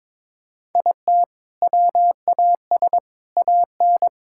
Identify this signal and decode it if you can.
Morse code, heard as a keyed tone, about 21 wpm, 700 Hz. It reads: IT WAS AN
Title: Sentences from Top 1000 Words - Review Part 6 - 22wpm